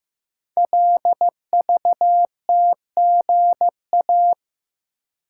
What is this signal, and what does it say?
Morse code, heard as a keyed tone, about 15 wpm, 700 Hz. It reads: LVTGA
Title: Code Groups - 5 Letters - 15wpm